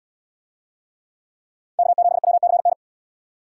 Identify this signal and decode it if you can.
Morse code, heard as a keyed tone, about 68 wpm, 700 Hz. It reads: K7WCN